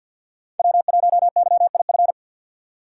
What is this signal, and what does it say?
Morse code, heard as a keyed tone, about 51 wpm, 700 Hz. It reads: W1YIF
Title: USA Call Signs - 6x Character Spacing - 50wpm